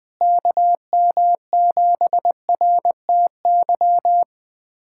Morse code, heard as a keyed tone, about 20 wpm, 700 Hz. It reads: KM7RTY
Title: USA Callsign DE Callsign 20wpm